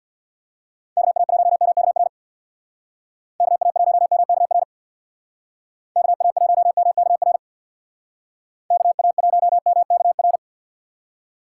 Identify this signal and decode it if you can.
Morse code, heard as a keyed tone, about 62 wpm, 700 Hz. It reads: KA1MKR KA1MKR KA1MKR KA1MKR
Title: Speed Racing - USA 2x3 Call Signs 50wpm